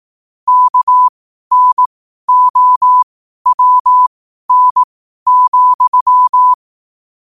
Morse code, heard as a keyed tone, about 18 wpm, spaced as for 12 wpm, 1000 Hz. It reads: KNOWN,
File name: qotdfiles / 12wpm